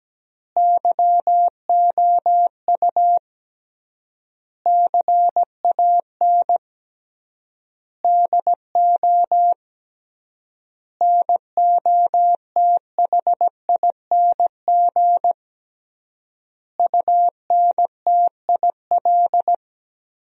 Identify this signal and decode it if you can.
Morse code, heard as a keyed tone, about 17 wpm, 700 Hz. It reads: YOU CAN DO NOTHING UNTIL
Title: 3x Word-Spacing - Sentences from Top 500 Words - 17wpm